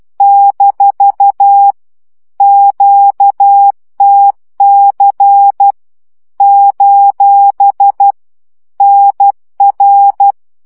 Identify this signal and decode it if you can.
Morse code, heard as a keyed tone, about 12 wpm, 800 Hz. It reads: - QTC : NR
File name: morze vizsga 2011-3 10